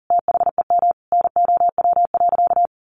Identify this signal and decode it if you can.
Morse code, heard as a keyed tone, about 40 wpm, 700 Hz. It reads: THEM NOW.
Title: Sentences from Top 1300 Words 40wpm